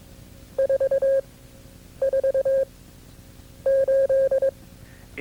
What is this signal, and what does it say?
Morse code, heard as a keyed tone, about 22 wpm, spaced as for 9 wpm, 540 Hz. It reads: 44 8
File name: CW-practice/KB6MT-tape2-side1